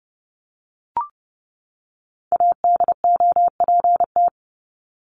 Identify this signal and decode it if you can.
Morse code, heard as a keyed tone, about 29 wpm, 700 Hz. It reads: ADOPT
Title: ED Suffix - Top 100 Words with Base Words 30wpm